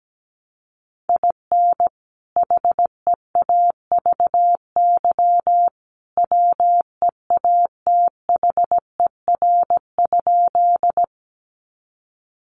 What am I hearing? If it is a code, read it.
Morse code, heard as a keyed tone, about 17 wpm, 700 Hz. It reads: IN HEAVY WEATHER?
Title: Sentences from Top 1100 Words 17wpm